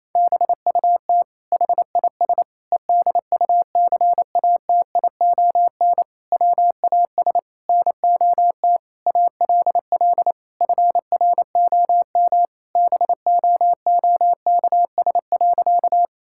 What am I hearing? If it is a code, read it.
Morse code, heard as a keyed tone, about 27 wpm, 700 Hz. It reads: BUT HIS EDUCATION WAS NOT ALL FROM BOOKS.
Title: Sentences from Top 1000 Words - Review Part 6 - 28wpm